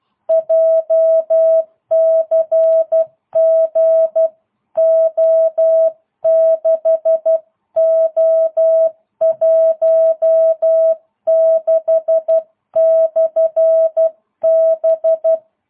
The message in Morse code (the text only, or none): JCG O6O16/B